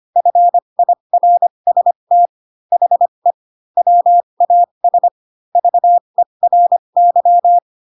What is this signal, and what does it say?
Morse code, heard as a keyed tone, about 25 wpm, spaced as for 22 wpm, 700 Hz. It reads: FIRST HE WAS VERY
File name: Pinocchio / Pinocchio ch7 18 25 ik1hge